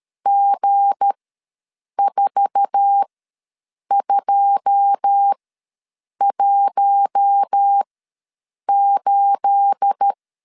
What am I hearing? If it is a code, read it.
Morse code, heard as a keyed tone, about 13 wpm, spaced as for 7 wpm, 800 Hz. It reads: G4218
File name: LEZIONI / 9 286 ws 12wpm 3351